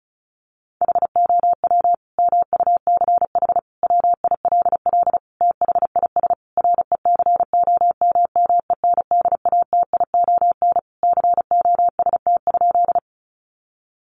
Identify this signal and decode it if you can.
Morse code, heard as a keyed tone, about 34 wpm, 700 Hz. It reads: HOW MUCH WILL THIS RECOMMENDATION COST?
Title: Sentences from Top 1400 Words 35wpm